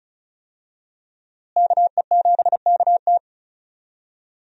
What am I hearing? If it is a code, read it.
Morse code, heard as a keyed tone, about 34 wpm, 700 Hz. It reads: KE7KT
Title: USA Call Signs - 5 Characters - Repeated 3x - 35wpm